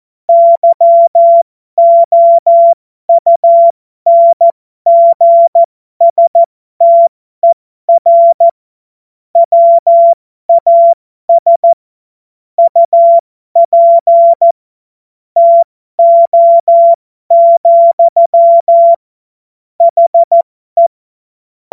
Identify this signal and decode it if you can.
Morse code, heard as a keyed tone, about 14 wpm, spaced as for 12 wpm, 680 Hz. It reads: YOUNGSTER WAS UP TO, HE
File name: Morsekurs / Witze